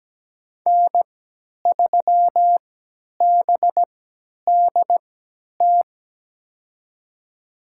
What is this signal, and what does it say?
Morse code, heard as a keyed tone, about 17 wpm, spaced as for 10 wpm, 700 Hz. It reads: N3BDT
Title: USA Call Signs - 5 Characters - 3x Character Spacing - 17wpm